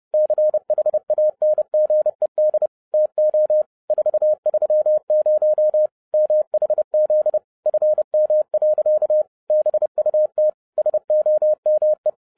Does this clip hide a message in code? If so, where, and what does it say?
Yes: Morse code from 0.1 to 12.1 s, heard as a keyed tone, about 30 wpm, 610 Hz. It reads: CHANGED TO 430 MHZ FM. BUT SOME